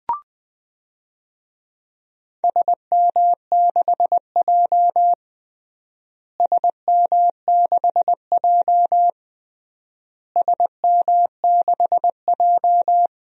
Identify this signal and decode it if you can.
Morse code, heard as a keyed tone, about 21 wpm, 700 Hz. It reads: SM6J SM6J SM6J
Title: DXCC-100 Call Signs - 4 Characters - Repeated 3x - 20wpm